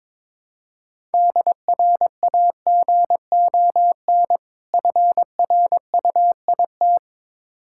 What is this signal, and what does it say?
Morse code, heard as a keyed tone, about 22 wpm, 690 Hz. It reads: DRAGON FRUIT
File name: Difficult Words with Hints - 22wpm